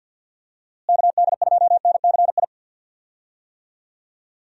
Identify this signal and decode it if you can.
Morse code, heard as a keyed tone, about 51 wpm, 700 Hz. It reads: KD2NCI
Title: USA Call Signs - 6 Characters - Repeated 2x - 50wpm